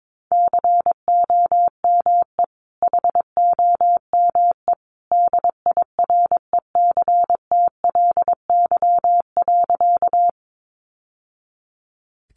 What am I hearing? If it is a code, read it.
Morse code, heard as a keyed tone, about 22 wpm, 700 Hz. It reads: COME HOME DIRECTLY.
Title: Sentences from Top 800 Words 22wpm